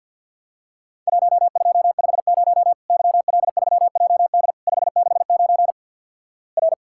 Morse code, heard as a keyed tone, about 50 wpm, 700 Hz. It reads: 1250 YL3JD 569 R
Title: Worked All Europe DX Contest - QTC Exchange with Repeats - 50wpm